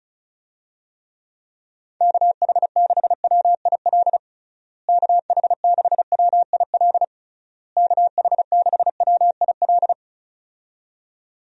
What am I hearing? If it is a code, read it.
Morse code, heard as a keyed tone, about 36 wpm, 700 Hz. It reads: KH6WIL KH6WIL KH6WIL
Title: USA Call Signs - 6 Characters - Repeated 3x - 35wpm